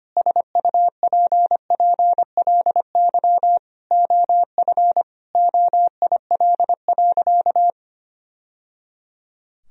Morse code, heard as a keyed tone, about 25 wpm, 700 Hz. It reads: SUPPLY OF OIL.
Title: Sentences from Top 1000 Words - Review Part 4 - 25wpm